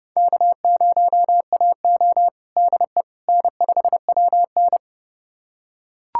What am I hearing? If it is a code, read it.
Morse code, heard as a keyed tone, about 29 wpm, 700 Hz. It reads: K0AO DE N5WN
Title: USA Callsign DE Callsign - 4 Characters - 30wpm